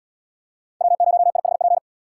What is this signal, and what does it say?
Morse code, heard as a keyed tone, about 73 wpm, 700 Hz. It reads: W0IHP